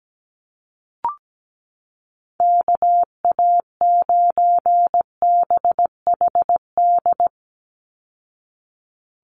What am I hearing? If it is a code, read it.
Morse code, heard as a keyed tone, about 17 wpm, 700 Hz. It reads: KA9BHD
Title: Call Signs Repeated 2x - 17wpm